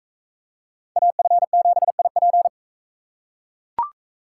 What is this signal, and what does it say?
Morse code, heard as a keyed tone, about 40 wpm, 700 Hz. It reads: AF7IP